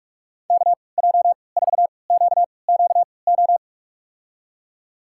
Morse code, heard as a keyed tone, about 46 wpm, spaced as for 25 wpm, 700 Hz. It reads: KJ4QQO
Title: USA Call Signs - 3x Character Spacing - 45wpm